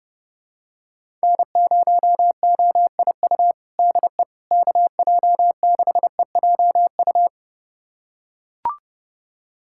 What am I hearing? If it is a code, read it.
Morse code, heard as a keyed tone, about 30 wpm, 700 Hz. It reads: N0OIU DE KJ6EJU
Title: USA Callsign DE Callsign 30wpm